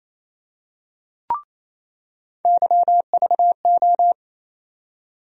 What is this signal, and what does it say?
Morse code, heard as a keyed tone, about 28 wpm, 700 Hz. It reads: YVO
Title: USA Call Sign Postfixes - 3 Letters 28wpm